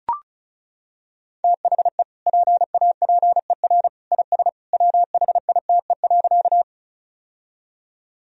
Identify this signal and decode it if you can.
Morse code, heard as a keyed tone, about 34 wpm, 700 Hz. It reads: THE PAPER IS WHITE.
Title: Sentences from Top 1000 Words - Review Part 1 - 35wpm